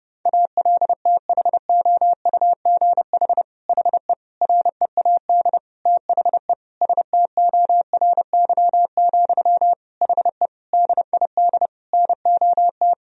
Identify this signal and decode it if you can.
Morse code, heard as a keyed tone, about 30 wpm, 700 Hz. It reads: ALTHOUGH HE READ THE STORY, HE DID NOT
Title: Sentences from Top 1000 Words - Review Part 3 - 30wpm